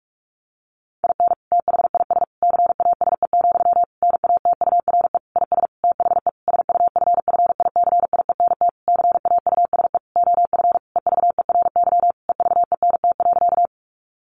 Morse code, heard as a keyed tone, about 44 wpm, 700 Hz. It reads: IN THIS CASE, NATURE IS THE SUFFICIENT CAUSE OF EVERY EVENT.
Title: Sentences from Top 1400 Words 45wpm